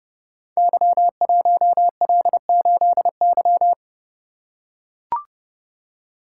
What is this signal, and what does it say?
Morse code, heard as a keyed tone, about 30 wpm, 700 Hz. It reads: Y1L8Y
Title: Code Groups - 5 Letter:Numbers - 30wpm